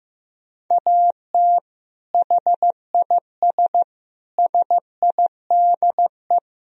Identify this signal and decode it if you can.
Morse code, heard as a keyed tone, about 15 wpm, 700 Hz. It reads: AT HIS SIDE
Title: Sentences from Top 1000 Words - Review Part 2 - 15wpm